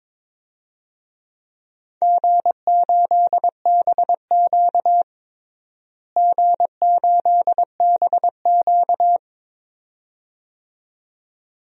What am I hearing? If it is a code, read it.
Morse code, heard as a keyed tone, about 22 wpm, 700 Hz. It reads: G8BQ G8BQ
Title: DXCC-100 Call Signs - 4 Characters - Repeated 2x - 22wpm